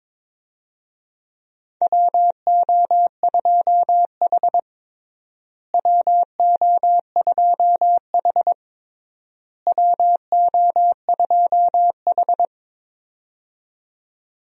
Morse code, heard as a keyed tone, about 22 wpm, 700 Hz. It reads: WO2H WO2H WO2H
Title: USA Call Signs - 4 Characters - Repeated 3x - 22wpm